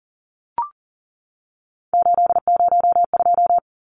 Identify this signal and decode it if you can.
Morse code, heard as a keyed tone, about 40 wpm, 700 Hz. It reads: 802
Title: Numbers - 3-Digit 40wpm